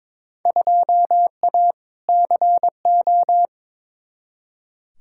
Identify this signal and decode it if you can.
Morse code, heard as a keyed tone, about 23 wpm, 700 Hz. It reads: 2A CO
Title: Field Day Exchanges - 22wpm